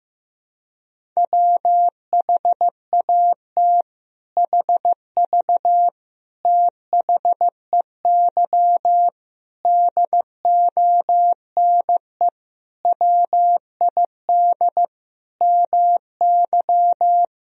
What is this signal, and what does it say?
Morse code, heard as a keyed tone, about 15 wpm, 700 Hz. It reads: WHAT HV THEY DONE WID MY
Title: Sentences from Top 500 Words and Abbreviations 15wpm